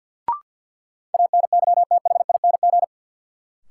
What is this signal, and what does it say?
Morse code, heard as a keyed tone, about 46 wpm, 700 Hz. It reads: ANYTHING
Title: Sentences from Top 1000 Words - New Word Drill - 50wpm